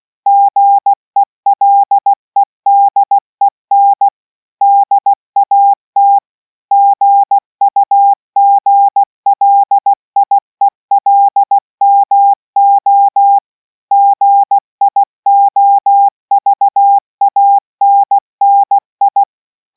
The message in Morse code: GELEDEN DAT GUGLIELMO GIOVANNI